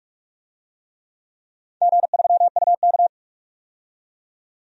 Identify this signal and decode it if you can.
Morse code, heard as a keyed tone, about 44 wpm, 700 Hz. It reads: G3UK
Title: DXCC-100 Call Signs - 4 Characters - 45wpm